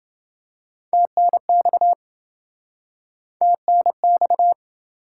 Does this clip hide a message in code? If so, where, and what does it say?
Yes: Morse code from 0.9 to 4.5 s, heard as a keyed tone, about 26 wpm, 700 Hz. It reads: TNX TNX